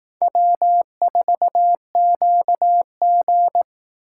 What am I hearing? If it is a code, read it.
Morse code, heard as a keyed tone, about 19 wpm, 700 Hz. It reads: W4QG